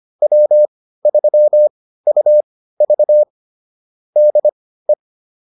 Morse code, heard as a keyed tone, about 25 wpm, spaced as for 15 wpm, 600 Hz. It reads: W3UV DE